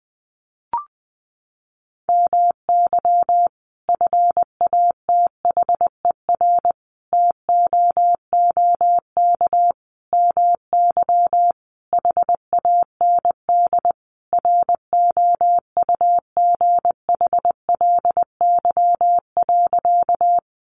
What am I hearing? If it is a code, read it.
Morse code, heard as a keyed tone, about 20 wpm, 700 Hz. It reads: MY FATHER TOOK MY HAND ROUGHLY.
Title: Sentences from Top 1500 Words 20wpm